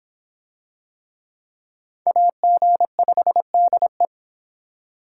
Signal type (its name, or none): Morse code